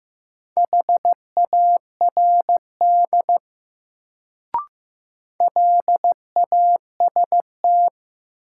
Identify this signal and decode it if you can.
Morse code, heard as a keyed tone, about 15 wpm, 700 Hz. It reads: HARD LAST
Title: Sentences from Top 300 Words - New Word Drill - 15wpm